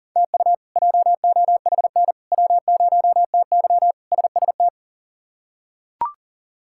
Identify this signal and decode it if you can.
Morse code, heard as a keyed tone, about 40 wpm, 700 Hz. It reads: TU JOHN W0TY SST
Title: K1USN SST QSO Practice - 40wpm